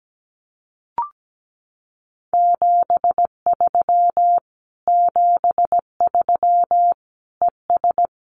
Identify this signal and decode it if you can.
Morse code, heard as a keyed tone, about 17 wpm, 700 Hz. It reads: 73 73 ES
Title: Common QSO Phrases Encore 17wpm